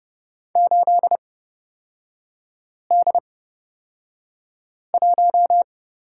Morse code, heard as a keyed tone, about 30 wpm, spaced as for 5 wpm, 700 Hz. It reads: 8D1
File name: Lesson 36e - BT - Three Characters 30wpm